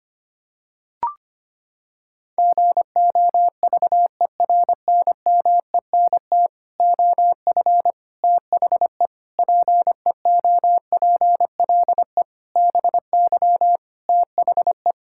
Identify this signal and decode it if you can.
Morse code, heard as a keyed tone, about 25 wpm, 700 Hz. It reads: GOVERNMENT OF THE PEOPLE BY THE
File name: Sets of 9 Words - 25wpm